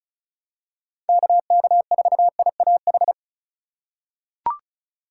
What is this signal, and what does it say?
Morse code, heard as a keyed tone, about 34 wpm, 700 Hz. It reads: KK4IAH
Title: USA Call Signs - 6 Characters - 2x Character Spacing - 35wpm